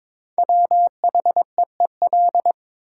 Morse code, heard as a keyed tone, about 21 wpm, 700 Hz. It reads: WHEEL
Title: Sentences from Top 500 Words - New Word Drill - 22wpm